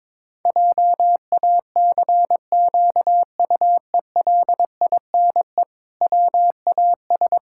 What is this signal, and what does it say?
Morse code, heard as a keyed tone, about 22 wpm, 700 Hz. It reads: JACQUELINE WAS
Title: Pangram Challenge v2 - 22wpm